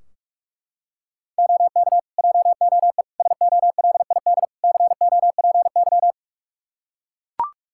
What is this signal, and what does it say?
Morse code, heard as a keyed tone, about 41 wpm, 700 Hz. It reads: OK JOE SOLID COPY
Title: Practice QSO - Exchange 3 - More Info - 45wpm